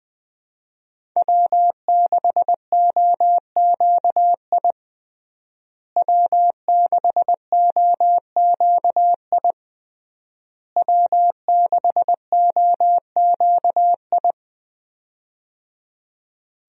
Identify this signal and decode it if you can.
Morse code, heard as a keyed tone, about 20 wpm, 700 Hz. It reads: W6OQI W6OQI W6OQI
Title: USA Call Signs - 5 Characters - Repeated 3x - 20wpm